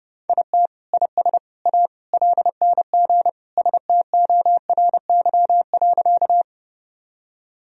Morse code, heard as a keyed tone, about 32 wpm, 700 Hz. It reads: IT IS A LNG STORY.
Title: Sentences from Top 300 Words and Abbreviations 30wpm